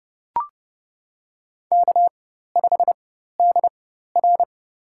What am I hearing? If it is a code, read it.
Morse code, heard as a keyed tone, about 30 wpm, spaced as for 14 wpm, 700 Hz. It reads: K5DR
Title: USA Call Signs - 4 Characters - 4x Character Spacing - 30wpm